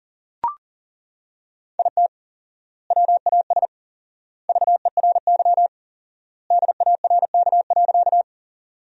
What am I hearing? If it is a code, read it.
Morse code, heard as a keyed tone, about 40 wpm, 700 Hz. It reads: IT WAS VERY DARK.
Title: 4x Word-Spacing - Sentences from Top 500 Words - 40wpm